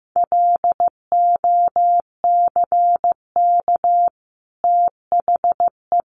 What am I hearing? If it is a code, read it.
Morse code, heard as a keyed tone, about 15 wpm, 700 Hz. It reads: LOCK THE
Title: Sentences from Top 1300 Words 15wpm